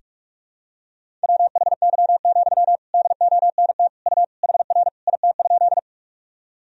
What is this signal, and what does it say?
Morse code, heard as a keyed tone, about 41 wpm, 700 Hz. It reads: WHY, DONT U HR IT?
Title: Sentences from Top 300 Words and Abbreviations 45wpm